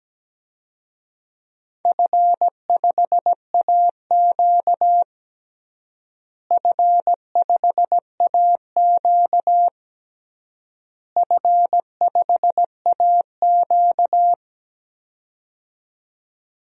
Morse code, heard as a keyed tone, about 17 wpm, 700 Hz. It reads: F5AQ F5AQ F5AQ
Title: DXCC-100 Call Signs - 4 Characters - Repeated 3x - 17wpm